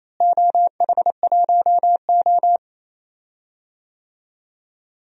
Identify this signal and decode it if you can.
Morse code, heard as a keyed tone, about 28 wpm, 700 Hz. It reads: OH1O
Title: DXCC-100 Call Signs - 4 Characters - Repeated 3x - 28wpm